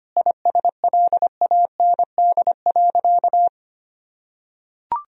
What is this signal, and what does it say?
Morse code, heard as a keyed tone, about 25 wpm, 700 Hz. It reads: ISLAND.
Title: Sentences from Top 1000 Words - Review Part 2 - 25wpm